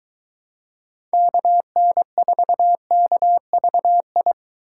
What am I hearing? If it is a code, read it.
Morse code, heard as a keyed tone, about 23 wpm, 700 Hz. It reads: KN4KVI